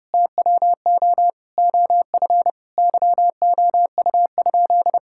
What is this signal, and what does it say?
Morse code, heard as a keyed tone, about 30 wpm, 700 Hz. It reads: TWO OF YOU?
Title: Sentences from Top 1000 Words - Review Part 6 - 30wpm